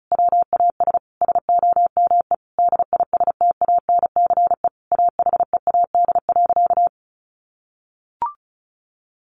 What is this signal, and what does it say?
Morse code, heard as a keyed tone, about 34 wpm, 700 Hz. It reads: WAS SOME DISTANCE AHEAD.
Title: Sentences from Top 1400 Words 35wpm